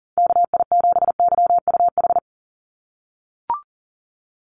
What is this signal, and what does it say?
Morse code, heard as a keyed tone, about 40 wpm, 700 Hz. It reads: KI7YUH
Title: USA Call Signs 40wpm